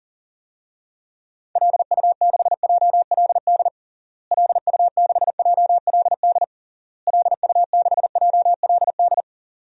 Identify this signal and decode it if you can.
Morse code, heard as a keyed tone, about 37 wpm, 700 Hz. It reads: LU6JLD LU6JLD LU6JLD
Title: DXCC-100 Call Signs - 6 Characters - Repeated 3x - 40wpm